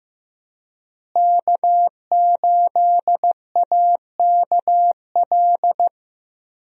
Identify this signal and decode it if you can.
Morse code, heard as a keyed tone, about 15 wpm, 700 Hz. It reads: K8AKL
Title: USA Call Signs - 5 Characters - Repeated 2x - 15wpm